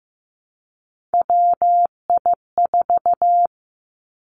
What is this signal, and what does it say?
Morse code, heard as a keyed tone, about 15 wpm, 700 Hz. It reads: WI4
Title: Three Challenging Letter-Number Combinations 15wpm